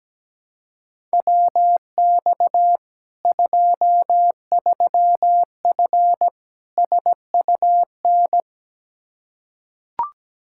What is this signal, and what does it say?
Morse code, heard as a keyed tone, about 17 wpm, 700 Hz. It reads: WX 23F SUN
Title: Practice QSO - Exchange 2 - Info - 17wpm